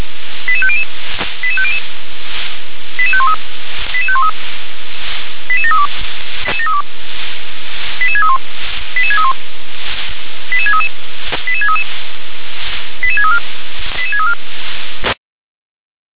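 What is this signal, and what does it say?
Five-tone selective call (ZVEI), heard as quick runs of five tones, tones of about 70 ms. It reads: 88500 88500 88513 88513 88732 88631 88631 88633 88633 88634 88634
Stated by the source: ZVEI five-tone selective call received by SDR.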